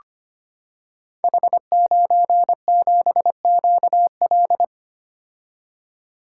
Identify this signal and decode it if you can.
Morse code, heard as a keyed tone, about 25 wpm, 700 Hz. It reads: H97QL
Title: Code Groups - 5 Letter:Numbers - 25wpm